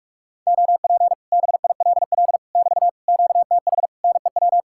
Morse code, heard as a keyed tone, about 44 wpm, 700 Hz. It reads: OP BILL <BT> QTH NEW